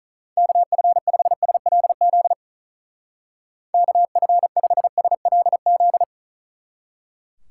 Morse code, heard as a keyed tone, about 42 wpm, 700 Hz. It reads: KF5SLZ KF5SLZ
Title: Speed Racing - USA 2x3 Call Signs 35wpm